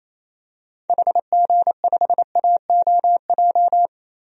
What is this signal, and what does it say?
Morse code, heard as a keyed tone, about 28 wpm, 700 Hz. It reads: HG5AOJ
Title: DXCC-100 Call Signs - 6 Characters - 28wpm